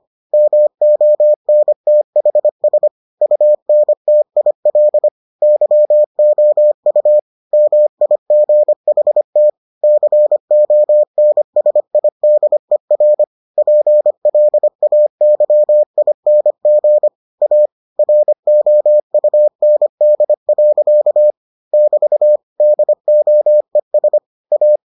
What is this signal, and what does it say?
Morse code, heard as a keyed tone, about 25 wpm, 600 Hz. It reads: MONTHS UNTIL YOU MIGHT CONSIDER PLAYING A ROUND. <BT> DOES A